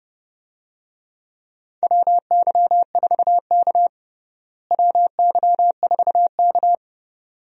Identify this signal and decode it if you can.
Morse code, heard as a keyed tone, about 30 wpm, 700 Hz. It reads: WY4K WY4K
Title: USA Call Signs - 4 Characters - Repeated 2x - 30wpm